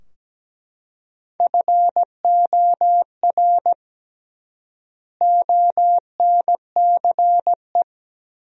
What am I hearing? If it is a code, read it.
Morse code, heard as a keyed tone, about 17 wpm, 700 Hz. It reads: FOR ONCE
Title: 3x Word-Spacing - Sentences from Top 300 Words - 17wpm